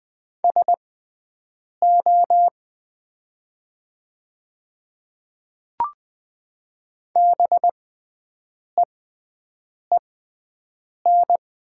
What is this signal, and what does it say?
Morse code, heard as a keyed tone, about 20 wpm, spaced as for 7 wpm, 700 Hz. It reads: SO BEEN
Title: 6x Character Spacing - Top 100 Words - 20wpm